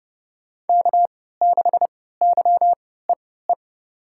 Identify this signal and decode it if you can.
Morse code, heard as a keyed tone, about 30 wpm, spaced as for 17 wpm, 700 Hz. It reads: K6YEE
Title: USA Call Signs - 5 Characters - 3x Character Spacing - 30wpm